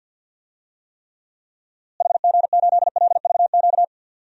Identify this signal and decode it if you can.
Morse code, heard as a keyed tone, about 51 wpm, 700 Hz. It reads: HG8LVQ
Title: DXCC-100 Call Signs - 6 Characters - Repeated 3x - 50wpm